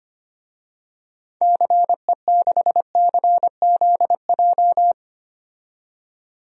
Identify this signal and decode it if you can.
Morse code, heard as a keyed tone, about 25 wpm, 700 Hz. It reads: CE6CZJ
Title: DXCC-100 Call Signs - 6 Characters - 25wpm